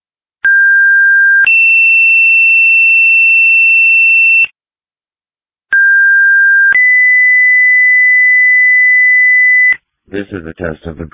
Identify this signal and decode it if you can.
Two-tone page, heard as a one-second tone, then a three-second tone, about 1600 then 2700 Hz, about 1600 then 2000 Hz, with speech. It is heard from 0.4 to 9.7 s.
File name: dispatch / Sumner Fire 2026 07 09 19 06 00